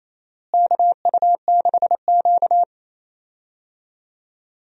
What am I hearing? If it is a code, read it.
Morse code, heard as a keyed tone, about 28 wpm, 700 Hz. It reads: KU6Q
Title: USA Call Signs - 4 Characters - Repeated 3x - 28wpm